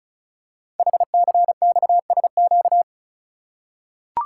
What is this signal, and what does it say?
Morse code, heard as a keyed tone, about 36 wpm, 700 Hz. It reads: HCXSQ
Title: Code Groups - 5 Letters - 35wpm